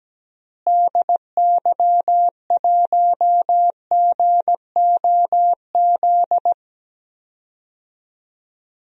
Morse code, heard as a keyed tone, about 17 wpm, 700 Hz. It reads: DY1GOZ
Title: DXCC-100 Call Signs - 6 Characters - Repeated 2x - 17wpm